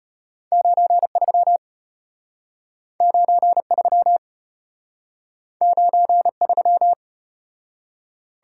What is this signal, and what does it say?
Morse code, heard as a keyed tone, about 34 wpm, 700 Hz. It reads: 93 93 93